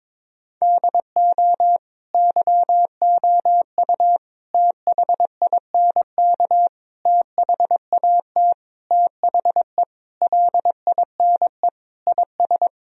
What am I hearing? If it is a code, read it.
Morse code, heard as a keyed tone, about 22 wpm, 700 Hz. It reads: DO YOU THINK THAT THE LINE IS
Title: Sentences from Top 1000 Words - Review Part 1 - 22wpm